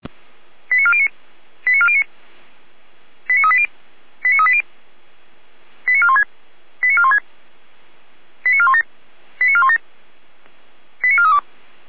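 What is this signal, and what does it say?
Five-tone selective call (ZVEI), heard as quick runs of five tones, tones of about 70 ms. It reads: 89409 89409 89380 89380 89416 89416 89417 89417 89432
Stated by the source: ZVEI five-tone selective call received by SDR.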